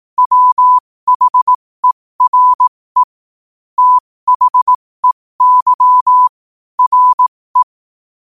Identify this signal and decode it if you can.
Morse code, heard as a keyed tone, about 18 wpm, spaced as for 15 wpm, 1000 Hz. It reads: WHERE THEY RE
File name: qotdfiles / 15wpm